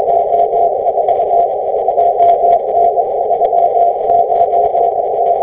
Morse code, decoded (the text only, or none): none